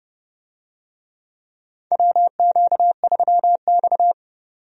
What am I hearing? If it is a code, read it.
Morse code, heard as a keyed tone, about 30 wpm, 700 Hz. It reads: WQ3X
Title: USA Call Signs - 4 Characters - 30wpm